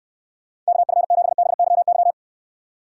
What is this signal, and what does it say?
Morse code, heard as a keyed tone, about 69 wpm, 700 Hz. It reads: KF7BYY